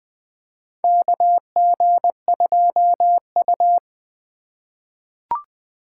Morse code, heard as a keyed tone, about 20 wpm, 700 Hz. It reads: KG2U
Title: USA Call Signs - 4 Characters - 3x Character Spacing - 20wpm